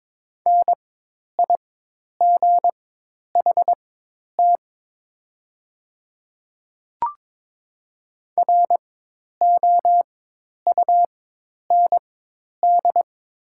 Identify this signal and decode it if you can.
Morse code, heard as a keyed tone, about 22 wpm, spaced as for 10 wpm, 700 Hz. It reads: NIGHT ROUND
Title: 4x Character Spacing - Top 300 Words - 22wpm